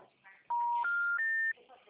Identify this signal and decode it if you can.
Special information tone, with speech.
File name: out- -2004- - - .435